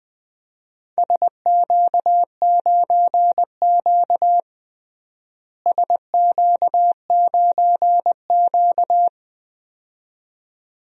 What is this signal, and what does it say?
Morse code, heard as a keyed tone, about 20 wpm, 700 Hz. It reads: SQ9Q SQ9Q
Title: DXCC-100 Call Signs - 4 Characters - Repeated 2x - 20wpm